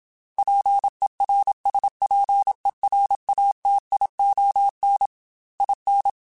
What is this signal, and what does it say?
Morse code, heard as a keyed tone, about 27 wpm, 800 Hz. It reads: PERSPERATION IN
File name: Quotes / Quotes-mp3s / Persperation @25